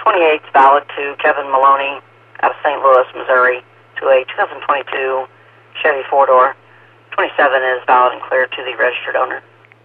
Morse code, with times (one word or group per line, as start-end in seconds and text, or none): none